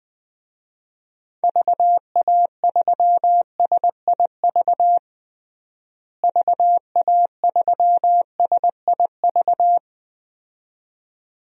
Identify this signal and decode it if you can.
Morse code, heard as a keyed tone, about 20 wpm, 700 Hz. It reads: VA3SIV VA3SIV
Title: DXCC-100 Call Signs - 6 Characters - Repeated 2x - 20wpm